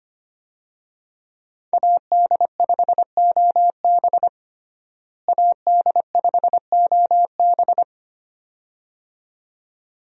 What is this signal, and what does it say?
Morse code, heard as a keyed tone, about 25 wpm, 690 Hz. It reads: AD5OB AD5OB
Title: USA Call Signs - 5 Characters - Repeated 2x - 25wpm